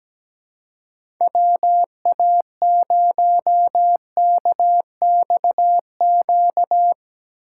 Morse code, heard as a keyed tone, about 17 wpm, 700 Hz. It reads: WA0KXQ
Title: USA Call Signs - 6 Characters - 6x Character Spacing - 17wpm